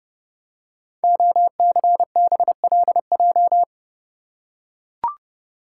Morse code, heard as a keyed tone, about 30 wpm, 700 Hz. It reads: OCBLJ
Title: Code Groups - 5 Letters - 30wpm